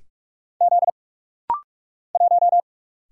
Morse code, heard as a keyed tone, about 44 wpm, 700 Hz. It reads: Z 1
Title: Single Letter-Number - Rapid-Fire 45wpm